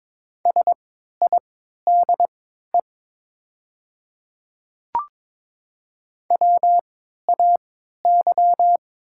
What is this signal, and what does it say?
Morse code, heard as a keyed tone, about 22 wpm, spaced as for 12 wpm, 700 Hz. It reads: SIDE WAY
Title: 3x Character Spacing - Top 100 Words - 22wpm